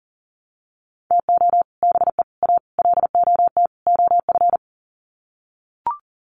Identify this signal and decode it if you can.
Morse code, heard as a keyed tone, about 40 wpm, 700 Hz. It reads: TO BE A LOT OF
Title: Sets of 5 Words Encore 40wpm